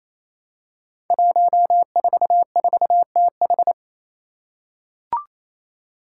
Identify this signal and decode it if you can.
Morse code, heard as a keyed tone, about 27 wpm, 700 Hz. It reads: 144TH
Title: Five Challenging Letter-Number Combinations 28wpm